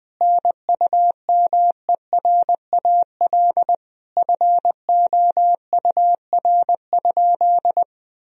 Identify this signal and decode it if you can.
Morse code, heard as a keyed tone, about 20 wpm, 700 Hz. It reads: NUMERAL FOUR?
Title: Sentences from Top 1000 Words - Review Part 2 - 20wpm